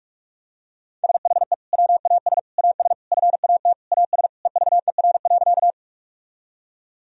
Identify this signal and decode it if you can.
Morse code, heard as a keyed tone, about 46 wpm, 700 Hz. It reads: SHE WAS AS FAT AS EVER.
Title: Sentences from Top 1000 Words - Review Part 4 - 45wpm